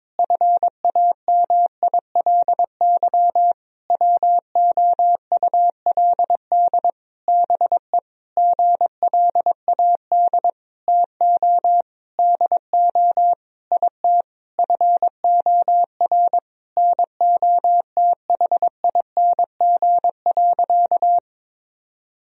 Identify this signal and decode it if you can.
Morse code, heard as a keyed tone, about 22 wpm, 700 Hz. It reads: FAMILY WOULD BE GLAD TO DO IT FOR NOTHING.
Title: Sentences from Top 1000 Words - Review Part 3 - 22wpm